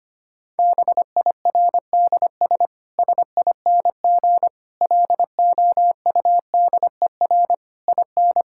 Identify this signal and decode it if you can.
Morse code, heard as a keyed tone, about 25 wpm, 700 Hz. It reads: BIRDS SING LOUDER IN
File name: practice / 25wpm / Difficult Sentences with Hints - 25wpm